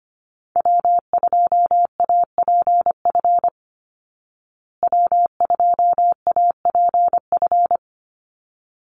Morse code, heard as a keyed tone, about 25 wpm, 700 Hz. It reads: W2APF W2APF
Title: Call Signs Repeated 3x - 25wpm